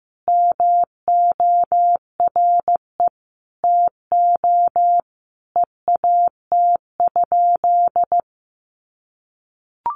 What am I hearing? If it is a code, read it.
Morse code, heard as a keyed tone, about 14 wpm, 700 Hz. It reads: MORE TO EAT?
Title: Sentences from Top 300 Words 15wpm